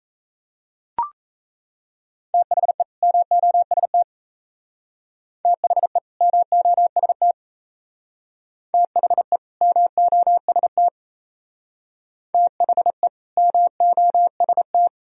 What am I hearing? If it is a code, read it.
Morse code, heard as a keyed tone, about 35 wpm, 700 Hz. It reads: THE MOST THE MOST THE MOST THE MOST
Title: Speed-Racing - Sets of 2 Words 28wpm